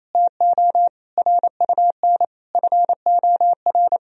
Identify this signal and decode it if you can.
Morse code, heard as a keyed tone, about 28 wpm, 700 Hz. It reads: TO RUN FOR
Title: Sentences from Top 1000 Words - Review Part 1 - 28wpm